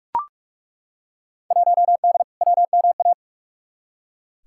Morse code, heard as a keyed tone, about 43 wpm, 700 Hz. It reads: 1D WMA
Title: Field Day Exchanges - 45wpm